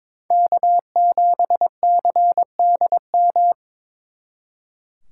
Morse code, heard as a keyed tone, about 26 wpm, 700 Hz. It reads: K7CDM
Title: Speed Racing - USA 1x3 Call Signs 22wpm